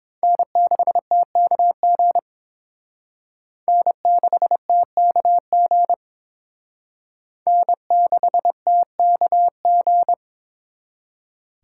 Morse code, heard as a keyed tone, about 27 wpm, 700 Hz. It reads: N6TKG N6TKG N6TKG
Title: Speed Racing - USA 1x3 Call Signs 22wpm